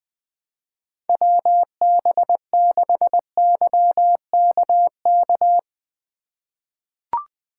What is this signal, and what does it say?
Morse code, heard as a keyed tone, about 20 wpm, 700 Hz. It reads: WB6YKK
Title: USA Call Signs - 4x Character Spacing - 20wpm